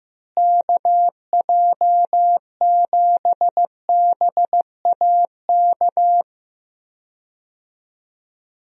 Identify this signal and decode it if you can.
Morse code, heard as a keyed tone, about 15 wpm, 700 Hz. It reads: KJ7BAK
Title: USA Call Signs - 6 Characters - Repeated 2x - 15wpm